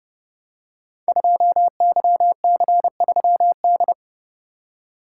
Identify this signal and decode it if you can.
Morse code, heard as a keyed tone, about 30 wpm, 700 Hz. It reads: 2YC3D
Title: Code Groups - 5 Letter:Numbers - 30wpm